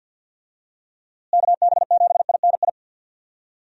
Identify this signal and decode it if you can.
Morse code, heard as a keyed tone, about 50 wpm, 700 Hz. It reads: KB7INI
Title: USA Call Signs - 6 Characters - Repeated 2x - 50wpm